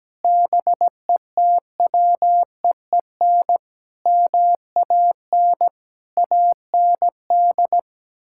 Morse code, heard as a keyed tone, about 17 wpm, 700 Hz. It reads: BETWEEN MAN AND
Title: Sentences from Top 1600 Words 17wpm